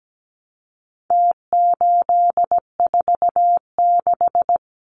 Morse code, heard as a keyed tone, about 17 wpm, 700 Hz. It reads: T846
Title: Numbers - Contest Style - 4-Digit 17wpm